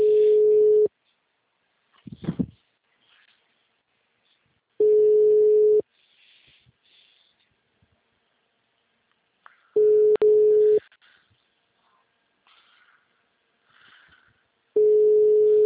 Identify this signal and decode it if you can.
Ringback tone (Europe, 425 Hz).